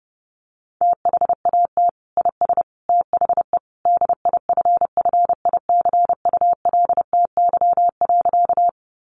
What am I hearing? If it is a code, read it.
Morse code, heard as a keyed tone, about 29 wpm, 700 Hz. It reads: THAT IS THE DIFFICULTY.
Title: Sentences from Top 1500 Words 30wpm